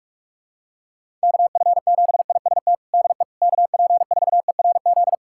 Morse code, heard as a keyed tone, about 46 wpm, 700 Hz. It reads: KF7IST DE KP4ERZ
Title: USA Callsign DE Callsign - 6 Characters - 45wpm